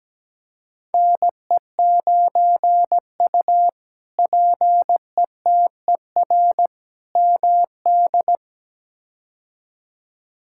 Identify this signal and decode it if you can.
Morse code, heard as a keyed tone, about 17 wpm, 700 Hz. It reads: NE9U PETER MD
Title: K1USN SST QSO Practice - 17wpm